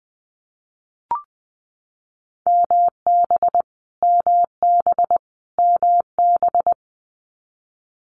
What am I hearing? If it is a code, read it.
Morse code, heard as a keyed tone, about 19 wpm, 700 Hz. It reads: MB MB MB
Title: Canadian Provinces and Territories - Repeated 3x - 20wpm